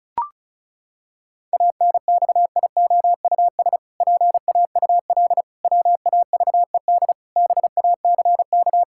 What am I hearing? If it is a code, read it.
Morse code, heard as a keyed tone, about 36 wpm, 700 Hz. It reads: ANXIOUS PAUL WAVED BACK
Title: Pangram Challenge v2 - 35wpm